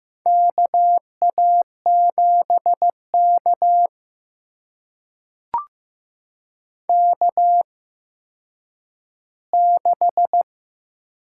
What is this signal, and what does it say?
Morse code, heard as a keyed tone, about 15 wpm, spaced as for 9 wpm, 700 Hz. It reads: KA7K K6
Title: USA Call Signs - 4 Characters - 8x Character Spacing - 15wpm